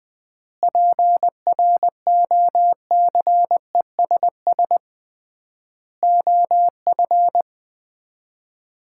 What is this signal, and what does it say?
Morse code, heard as a keyed tone, about 21 wpm, 700 Hz. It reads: PROCESS OF
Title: 3x Word-Spacing - Sets of 4 Words - 20wpm